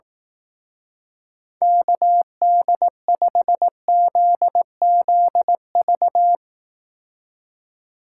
Morse code, heard as a keyed tone, about 18 wpm, 700 Hz. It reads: KD5ZZV
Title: Speed Racing - USA 2x3 Call Signs 15wpm